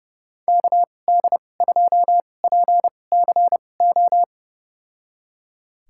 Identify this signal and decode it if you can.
Morse code, heard as a keyed tone, about 31 wpm, spaced as for 22 wpm, 700 Hz. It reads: KD2PCO
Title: USA Call Signs - 6 Characters - 2x Character Spacing - 30wpm